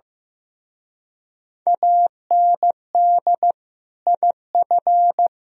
Morse code, heard as a keyed tone, about 16 wpm, 700 Hz. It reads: AND IF
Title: Speed-Racing - Sets of 3 Words 15wpm